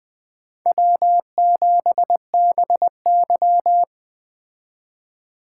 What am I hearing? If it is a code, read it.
Morse code, heard as a keyed tone, about 20 wpm, 700 Hz. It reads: W7BY
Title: Speed Racing - USA 1x2 Call Signs 17wpm